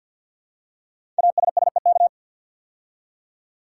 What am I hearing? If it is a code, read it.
Morse code, heard as a keyed tone, about 50 wpm, 690 Hz. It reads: ASSEK
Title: Code Groups - 5 Letters - 50wpm